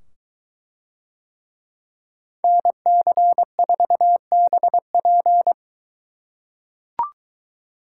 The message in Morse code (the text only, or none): NC4BP